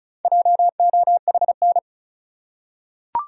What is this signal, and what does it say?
Morse code, heard as a keyed tone, about 34 wpm, 700 Hz. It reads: JOHN
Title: USA Common Names - Max 5 Letters - 35wpm